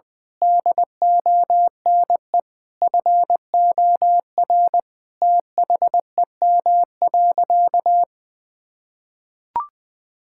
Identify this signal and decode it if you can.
Morse code, heard as a keyed tone, about 20 wpm, 700 Hz. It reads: DONE FOR THEM.
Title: Sentences from Top 1600 Words 20wpm